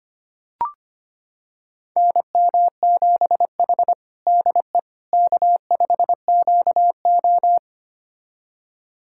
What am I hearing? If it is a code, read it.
Morse code, heard as a keyed tone, about 25 wpm, 700 Hz. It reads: NM7H DE K5QO
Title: USA Callsign DE Callsign - 4 Characters - 25wpm